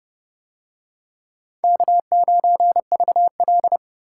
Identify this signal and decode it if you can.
Morse code, heard as a keyed tone, about 27 wpm, 700 Hz. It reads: K9VL